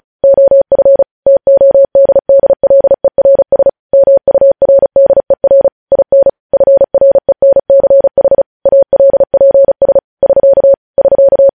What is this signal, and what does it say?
Morse code, heard as a keyed tone, about 36 wpm, 550 Hz. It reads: OF TODDLERS MURDER IN FRENCH ALPS <SK> <SK>
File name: podcast / News headlines-1x-35x35